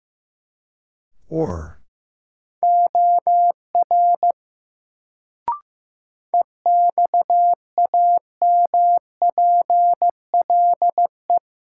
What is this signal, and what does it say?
Morse code, heard as a keyed tone, about 16 wpm, 700 Hz. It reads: OR EXAMPLE
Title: Sets of 2 Words - New Word Drill - 15wpm